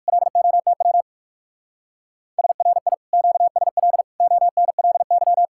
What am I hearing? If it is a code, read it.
Morse code, heard as a keyed tone, about 46 wpm, 700 Hz. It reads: LOTW SRI QSL ONLY
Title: Speed-Racing Common QSO Phrases Encore - 45wpm